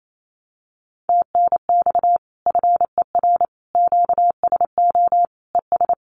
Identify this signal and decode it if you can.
Morse code, heard as a keyed tone, about 28 wpm, 700 Hz. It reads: TNX FER QSO ES
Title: Common QSO Phrases Encore 28wpm